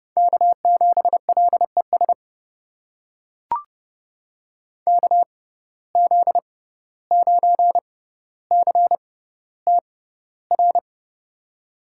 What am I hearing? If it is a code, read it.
Morse code, heard as a keyed tone, about 30 wpm, spaced as for 15 wpm, 700 Hz. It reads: K7LES KZ9CTR
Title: USA Call Signs - 6x Character Spacing - 30wpm